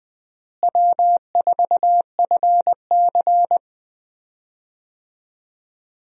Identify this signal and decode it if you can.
Morse code, heard as a keyed tone, about 20 wpm, 700 Hz. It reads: W4FC
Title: USA Call Signs - 4 Characters - Repeated 2x - 20wpm